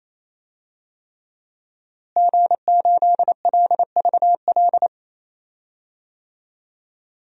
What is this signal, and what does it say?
Morse code, heard as a keyed tone, about 28 wpm, 700 Hz. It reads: G8LVL
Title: DXCC-100 Call Signs - 5 Characters - Repeated 3x - 28wpm